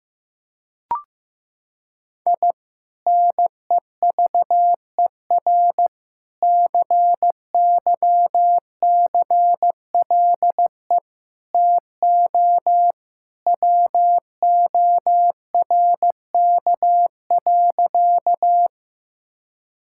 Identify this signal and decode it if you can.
Morse code, heard as a keyed tone, about 15 wpm, 700 Hz. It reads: I NEVER CYCLE TO WORK.
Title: Sentences from Top 1000 Words - Review Part 6 - 15wpm